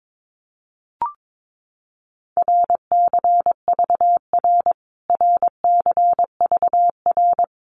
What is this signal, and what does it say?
Morse code, heard as a keyed tone, about 22 wpm, 700 Hz. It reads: RCVR RCVR
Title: Abbreviations Repeated 3x - 22wpm